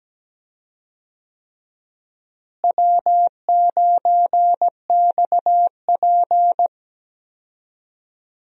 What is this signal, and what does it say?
Morse code, heard as a keyed tone, about 17 wpm, 700 Hz. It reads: W9XP
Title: USA Call Signs - 4 Characters - Repeated 3x - 17wpm